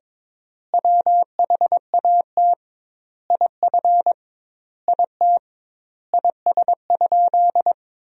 Morse code, heard as a keyed tone, about 21 wpm, 700 Hz. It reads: WHAT IF IT IS?
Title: 2x Word-Spacing - Sentences from Top 100 Words - 22wpm